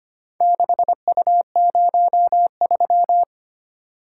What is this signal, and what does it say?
Morse code, heard as a keyed tone, about 25 wpm, 700 Hz. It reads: 6U03
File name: practice / 25wpm / Four Letter-Number Combinations - 25wpm